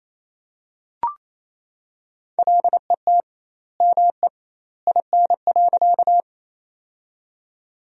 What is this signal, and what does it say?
Morse code, heard as a keyed tone, about 27 wpm, 700 Hz. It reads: LET ME IN.
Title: 2x Word-Spacing - Sentences from Top 300 Words - 28wpm